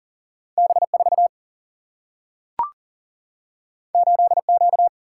Morse code, heard as a keyed tone, about 40 wpm, 700 Hz. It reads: B 4 8 Q
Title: Single-Letter Single-Number 40wpm